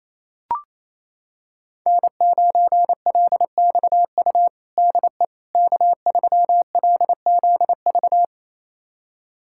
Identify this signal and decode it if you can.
Morse code, heard as a keyed tone, about 28 wpm, 700 Hz. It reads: N9LXU DE K3LZV
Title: USA Callsign DE Callsign - 5 Characters - 28wpm